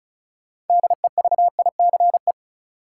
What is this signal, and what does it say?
Morse code, heard as a keyed tone, about 36 wpm, 700 Hz. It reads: DEVICE
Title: Sentences from Top 1500 Words - New Word Drill - 35wpm